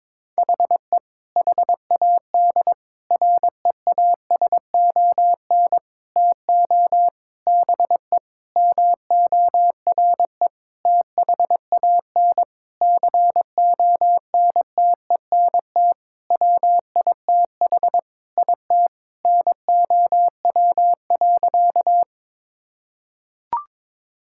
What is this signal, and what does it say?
Morse code, heard as a keyed tone, about 22 wpm, 700 Hz. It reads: HE HAD REASON TO BE MORE THAN CONTENT WITH IT NOW.
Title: Sentences from Top 1000 Words - Review Part 6 - 22wpm